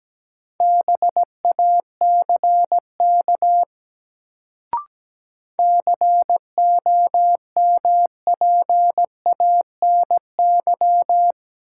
Morse code, heard as a keyed tone, about 17 wpm, 700 Hz. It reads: BACK COMPANY
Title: Sets of 2 Words - New Word Drill - 17wpm